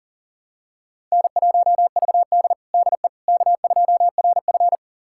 Morse code, heard as a keyed tone, about 40 wpm, 700 Hz. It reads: N1VD DE K2RF
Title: USA Callsign DE Callsign - 4 Characters - 40wpm